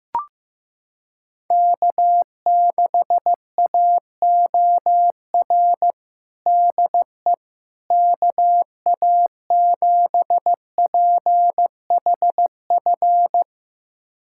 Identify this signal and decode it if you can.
Morse code, heard as a keyed tone, about 16 wpm, 700 Hz. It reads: K6AOR DE KA7PHF
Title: USA Callsign DE Callsign 15wpm